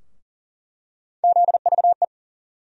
Morse code, heard as a keyed tone, about 40 wpm, 700 Hz. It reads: ZVE
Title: USA Call Sign Postfixes - 3 Letters 40wpm